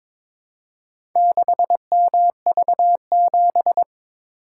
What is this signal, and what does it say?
Morse code, heard as a keyed tone, about 22 wpm, 700 Hz. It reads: 6MV7
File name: practice / 22wpm / Four Letter-Number Combinations - 22wpm